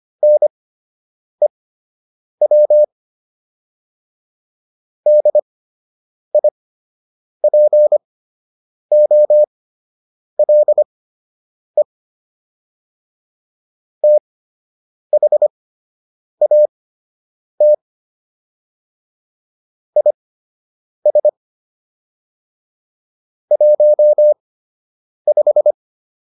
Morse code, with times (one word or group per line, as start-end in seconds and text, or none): 0.2-2.8 NEW
5.1-11.8 DIPOLE
14.0-17.7 THAT
20.0-21.3 IS
23.5-25.7 15